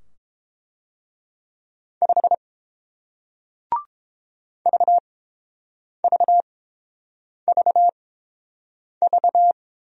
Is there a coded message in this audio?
Yes: Morse code from 2.0 to 9.5 s, heard as a keyed tone, about 30 wpm, spaced as for 7 wpm, 700 Hz. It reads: 5 VVVV